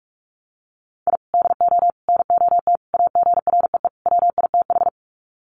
Morse code, heard as a keyed tone, about 44 wpm, 700 Hz. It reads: I DO NOT AGREE WITH
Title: Sets of 5 Words Encore 45wpm